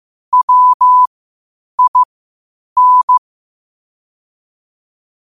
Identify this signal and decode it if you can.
Morse code, heard as a keyed tone, about 15 wpm, spaced as for 8 wpm, 1000 Hz. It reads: WIN
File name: qotdfiles / 8wpm